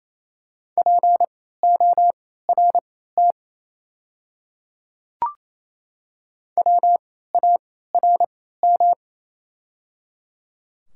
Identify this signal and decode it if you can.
Morse code, heard as a keyed tone, about 28 wpm, spaced as for 16 wpm, 690 Hz. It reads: PORT WARM
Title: 3x Character Spacing - Top 500 Words - 28wpm